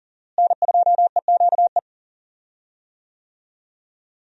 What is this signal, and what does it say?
Morse code, heard as a keyed tone, about 39 wpm, 700 Hz. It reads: N2EQE
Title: USA Call Signs - 5 Characters - Repeated 2x - 40wpm